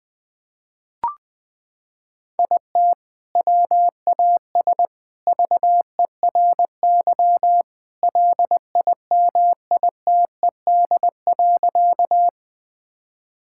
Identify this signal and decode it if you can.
Morse code, heard as a keyed tone, about 21 wpm, 700 Hz. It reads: IT WAS VERY LIMITED.
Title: Sentences from Top 1000 Words - Review Part 6 - 20wpm